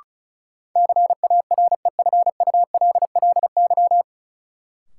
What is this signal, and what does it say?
Morse code, heard as a keyed tone, about 34 wpm, 700 Hz. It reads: CAREFULLY
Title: Sentences from Top 1300 Words - New Word Drill Alt - 35wpm